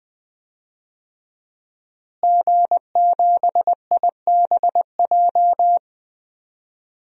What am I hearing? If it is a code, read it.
Morse code, heard as a keyed tone, about 19 wpm, 700 Hz. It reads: G7IBJ
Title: DXCC-100 Call Signs - 5 Characters - Repeated 3x - 20wpm